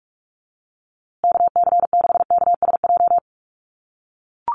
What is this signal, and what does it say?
Morse code, heard as a keyed tone, about 44 wpm, 700 Hz. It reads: KC6KSJ
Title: USA Call Signs Repeated 3x 45wpm